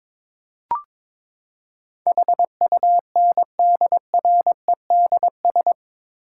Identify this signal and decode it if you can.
Morse code, heard as a keyed tone, about 22 wpm, 700 Hz. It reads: HUNDREDS
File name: practice / 22wpm / Build A Phrase - 5-Word Phrases - 22wpm